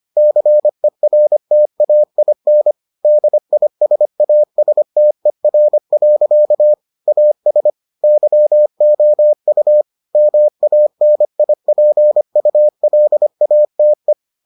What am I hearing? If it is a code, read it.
Morse code, heard as a keyed tone, about 25 wpm, 600 Hz. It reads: CERTAIN DISASTER. AS YOU MANIPULATE